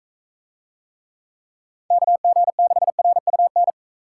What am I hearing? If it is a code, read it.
Morse code, heard as a keyed tone, about 41 wpm, 700 Hz. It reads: KG6RUN